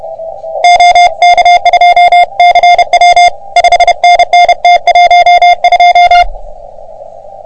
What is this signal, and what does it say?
Morse code, heard as a keyed tone, about 30 wpm, 720 Hz. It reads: OK2CW 5NN T12
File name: OM3PA PILUP CUC 388